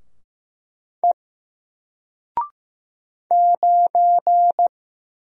Morse code, heard as a keyed tone, about 15 wpm, 700 Hz. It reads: E 9
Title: Single Letter-Number 15wpm